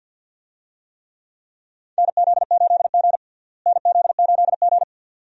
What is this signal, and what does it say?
Morse code, heard as a keyed tone, about 51 wpm, 700 Hz. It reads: NZ8G NZ8G
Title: USA Call Signs - 4 Characters - Repeated 2x - 50wpm